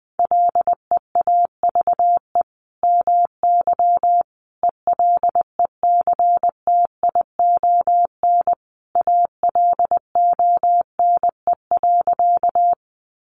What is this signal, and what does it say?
Morse code, heard as a keyed tone, about 20 wpm, 700 Hz. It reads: LEAVE MY ELECTION ALONE.
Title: Sentences from Top 1500 Words 20wpm